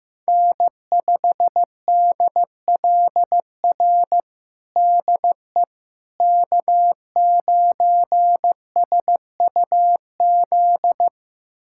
Morse code, heard as a keyed tone, about 15 wpm, 700 Hz. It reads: N5DLR DE K9SUZ
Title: USA Callsign DE Callsign - 5 Characters - 15wpm